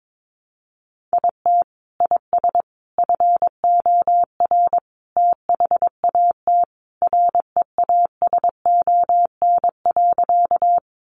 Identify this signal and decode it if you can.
Morse code, heard as a keyed tone, about 22 wpm, 700 Hz. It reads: IT IS FOR THAT REASON.
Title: Sentences from Top 700 Words 22wpm